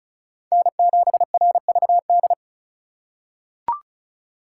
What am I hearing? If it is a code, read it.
Morse code, heard as a keyed tone, about 36 wpm, 700 Hz. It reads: N7RVD
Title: K1USN SST QSO Practice - 35wpm